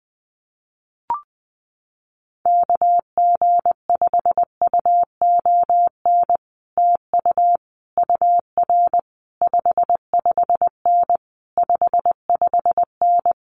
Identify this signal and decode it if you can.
Morse code, heard as a keyed tone, about 20 wpm, 700 Hz. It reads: KG5UON TU UR 55N 55N
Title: Common QSO Phrases Encore 20wpm